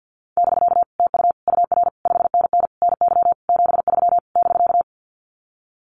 Morse code, heard as a keyed tone, about 50 wpm, 700 Hz. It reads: <BK> TU UR 5NN NY 73 <BK>
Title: Common QSO Phrases Encore 50wpm